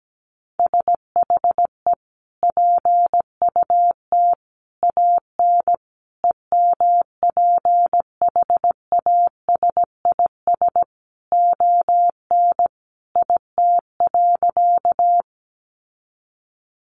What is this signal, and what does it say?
Morse code, heard as a keyed tone, about 17 wpm, 700 Hz. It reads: SHE PUT AN EMPHASIS ON IT.
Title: Sentences from Top 1400 Words 17wpm